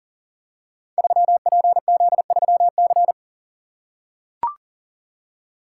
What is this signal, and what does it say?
Morse code, heard as a keyed tone, about 42 wpm, 690 Hz. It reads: 3PZ3C
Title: Code Groups - 5 Letter:Numbers - 40wpm